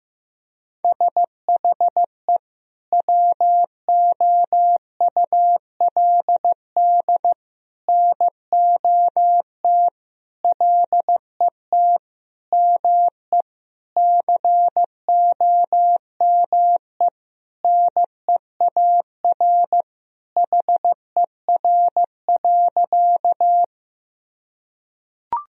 Morse code, heard as a keyed tone, about 16 wpm, 700 Hz. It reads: SHE WOULD NOT LET ME COME NEAR HER.
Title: Sentences from Top 1000 Words - Review Part 1 - 15wpm